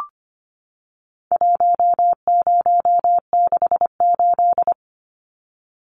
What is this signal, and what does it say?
Morse code, heard as a keyed tone, about 25 wpm, 700 Hz. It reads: 1068
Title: Numbers - 4-Digit 25wpm